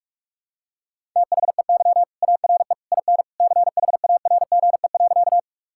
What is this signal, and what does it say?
Morse code, heard as a keyed tone, about 46 wpm, 700 Hz. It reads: THEY ARE IN CHARGE.
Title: Sentences from Top 1000 Words - Review Part 6 - 45wpm